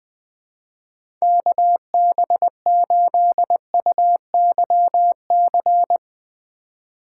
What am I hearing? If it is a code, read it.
Morse code, heard as a keyed tone, about 20 wpm, 700 Hz. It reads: KB8UYC
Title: USA Call Signs - 6 Characters - 3x Character Spacing - 20wpm